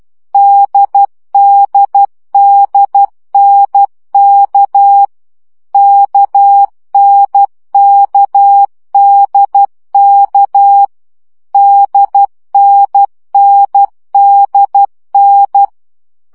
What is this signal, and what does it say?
Morse code, heard as a keyed tone, about 12 wpm, 800 Hz. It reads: DDDNK KNKDK DNNDN
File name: CW gyakorlas n d k 10 baud